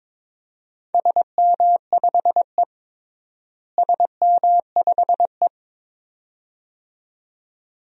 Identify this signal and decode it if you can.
Morse code, heard as a keyed tone, about 22 wpm, 700 Hz. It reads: SM5E SM5E
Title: DXCC-100 Call Signs - 4 Characters - Repeated 2x - 22wpm